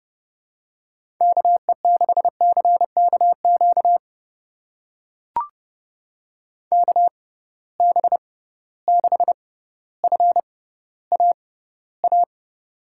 Morse code, heard as a keyed tone, about 30 wpm, spaced as for 15 wpm, 700 Hz. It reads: KE6CKQ KB6FAA
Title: USA Call Signs - 6 Characters - 6x Character Spacing - 30wpm